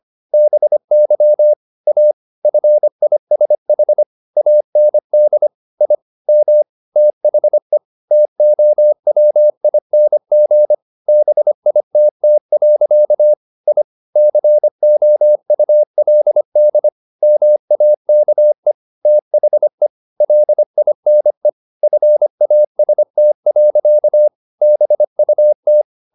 Morse code, heard as a keyed tone, about 25 wpm, 600 Hz. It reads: BY A FISH AND I M THE TOWING BITT. I COULD MAKE THE LINE FAST. BUT
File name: the old man and the sea part 1 25wpm